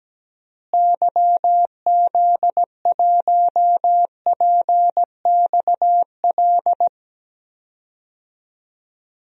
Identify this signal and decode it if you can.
Morse code, heard as a keyed tone, about 17 wpm, 700 Hz. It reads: YZ1PXL
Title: DXCC-100 Call Signs - 6 Characters - Repeated 3x - 17wpm